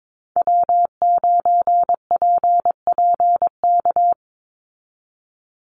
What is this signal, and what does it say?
Morse code, heard as a keyed tone, about 22 wpm, 700 Hz. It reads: W9PPK
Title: USA Call Signs 22wpm